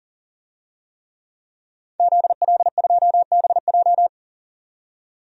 Morse code, heard as a keyed tone, about 40 wpm, 700 Hz. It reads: ZL2BJ
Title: DXCC-100 Call Signs - 5 Characters - 40wpm